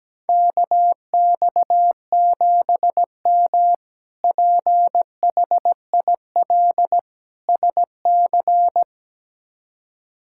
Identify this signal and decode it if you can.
Morse code, heard as a keyed tone, about 17 wpm, 700 Hz. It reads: KX7M PHIL SC
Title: K1USN SST QSO Practice - 17wpm